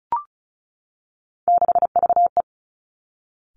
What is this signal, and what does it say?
Morse code, heard as a keyed tone, about 36 wpm, 700 Hz. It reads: 6VE
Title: Three Challenging Letter-Number Combinations 35wpm